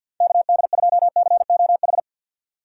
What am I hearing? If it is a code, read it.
Morse code, heard as a keyed tone, about 53 wpm, 700 Hz. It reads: KD2COH
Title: USA Call Signs - 6 Characters - 6x Character Spacing - 50wpm